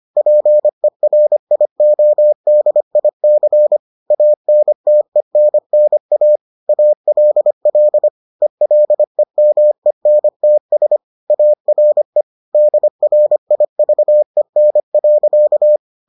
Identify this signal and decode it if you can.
Morse code, heard as a keyed tone, about 25 wpm, 600 Hz. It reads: PERIODIC ANTENNA ALL ELEMENTS ARE DRIVEN.